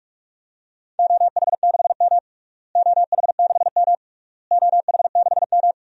Morse code, heard as a keyed tone, about 48 wpm, 700 Hz. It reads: OH6M OH6M OH6M
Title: DXCC-100 Call Signs - 4 Characters - Repeated 3x - 45wpm